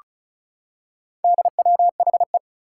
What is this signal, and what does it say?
Morse code, heard as a keyed tone, about 36 wpm, 700 Hz. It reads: DWHE
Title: Four Challenging Letter-Number Combinations 35wpm